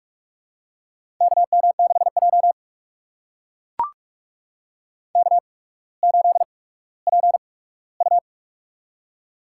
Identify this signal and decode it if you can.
Morse code, heard as a keyed tone, about 44 wpm, spaced as for 19 wpm, 700 Hz. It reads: KM6J K8PU
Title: USA Call Signs - 4 Characters - 8x Character Spacing - 45wpm